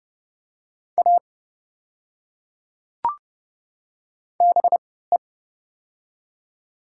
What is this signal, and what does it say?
Morse code, heard as a keyed tone, about 30 wpm, spaced as for 17 wpm, 700 Hz. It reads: A BE
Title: 3x Character Spacing - Top 100 Words - 30wpm